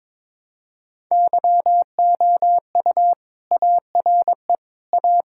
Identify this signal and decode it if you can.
Morse code, heard as a keyed tone, about 22 wpm, 700 Hz. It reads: YOU ARE A
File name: practice / 22wpm / Sentences with 3-Word Phrases - 22wpm